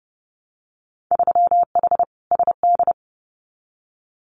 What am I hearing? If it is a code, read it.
Morse code, heard as a keyed tone, about 30 wpm, 700 Hz. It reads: 3H SD
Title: Winter Field Day Exchanges Encore 30wpm